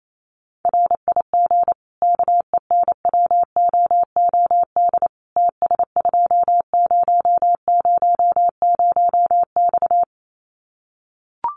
Morse code, heard as a keyed tone, about 28 wpm, 700 Hz. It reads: RIG KENWOOD TS2000X
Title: Common QSO Phrases 28wpm